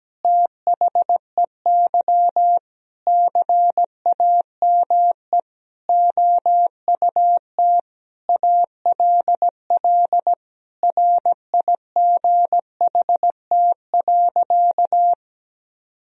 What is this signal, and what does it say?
Morse code, heard as a keyed tone, about 17 wpm, 700 Hz. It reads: THEY CAME OUT ALL RIGHT.
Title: Sentences from Top 1000 Words - Review Part 4 - 17wpm